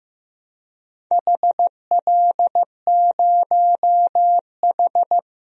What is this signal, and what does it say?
Morse code, heard as a keyed tone, about 15 wpm, 700 Hz. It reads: HL0H
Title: DXCC-100 Call Signs - 4 Characters - 15wpm